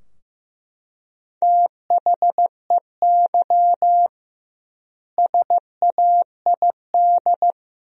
Morse code, heard as a keyed tone, about 16 wpm, 700 Hz. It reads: THEY SAID
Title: 2x Word-Spacing - Sentences from Top 100 Words - 15wpm